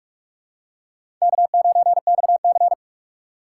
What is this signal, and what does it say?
Morse code, heard as a keyed tone, about 47 wpm, 700 Hz. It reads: K9XC
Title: USA Call Signs - 4 Characters - 3x Character Spacing - 45wpm